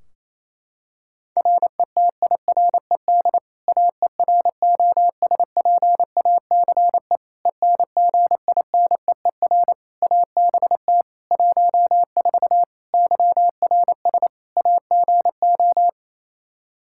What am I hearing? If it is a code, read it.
Morse code, heard as a keyed tone, about 28 wpm, 700 Hz. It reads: RETIRED AEROSPACE ENGINEER ABT 14 YRS AGO
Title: K1USN SST QSO Practice 28wpm